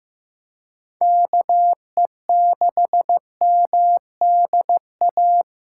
Morse code, heard as a keyed tone, about 15 wpm, 700 Hz. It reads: KE6MDA
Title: USA Call Signs - 4x Character Spacing - 15wpm